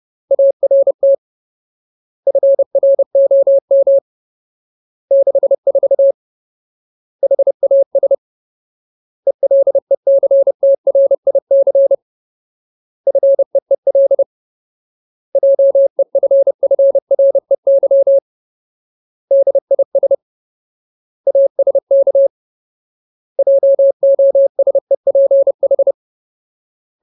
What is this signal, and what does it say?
Morse code, heard as a keyed tone, about 30 wpm, 550 Hz. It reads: ART FROM 64 HAS ELECTRIC FEEL JEFFREY DIS ASK JOSEPH
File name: Ultimate Word List / 30WPM Ultimate